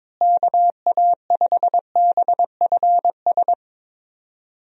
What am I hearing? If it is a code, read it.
Morse code, heard as a keyed tone, about 22 wpm, 700 Hz. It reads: KA5BFS
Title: USA Call Signs - 6 Characters - Repeated 2x - 22wpm